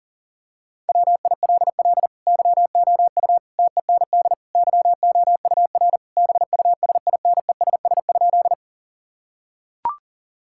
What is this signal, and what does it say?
Morse code, heard as a keyed tone, about 37 wpm, 700 Hz. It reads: WILL YOU TEND YOUR BUSINESS?
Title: Sentences from Top 1000 Words - Review Part 5 - 40wpm